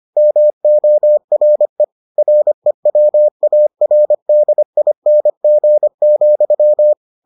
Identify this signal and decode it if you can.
Morse code, heard as a keyed tone, about 25 wpm, 600 Hz. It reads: MORE REWARDING,